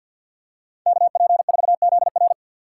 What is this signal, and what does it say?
Morse code, heard as a keyed tone, about 47 wpm, 690 Hz. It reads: KP4ZR
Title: USA Call Signs - 5 Characters - 2x Character Spacing - 50wpm